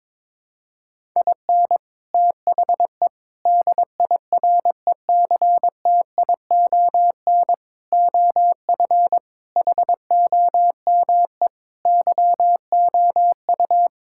Morse code, heard as a keyed tone, about 22 wpm, 700 Hz. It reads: IN THE DIRECTION OF HOME YOU
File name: practice / 22wpm / Sentences with 4-Word Phrases - 22wpm